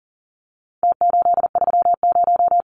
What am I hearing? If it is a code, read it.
Morse code, heard as a keyed tone, about 40 wpm, 700 Hz. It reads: T830
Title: Numbers - Contest Style - 4-Digit 40wpm